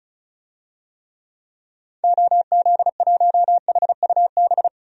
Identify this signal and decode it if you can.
Morse code, heard as a keyed tone, about 36 wpm, 700 Hz. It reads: OZ1HUB
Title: DXCC-100 Call Signs - 6 Characters - Repeated 2x - 35wpm